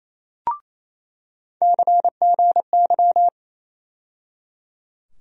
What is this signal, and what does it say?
Morse code, heard as a keyed tone, about 28 wpm, 700 Hz. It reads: CGY
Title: USA Call Sign Postfixes - 3 Letters 28wpm